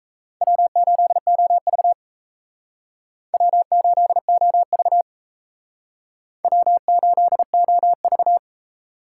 Morse code, heard as a keyed tone, about 37 wpm, 700 Hz. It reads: W8OV W8OV W8OV